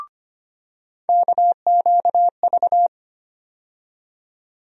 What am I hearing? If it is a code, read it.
Morse code, heard as a keyed tone, about 25 wpm, 700 Hz. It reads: KQV
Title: USA Call Sign Postfixes - 3 Letters 25wpm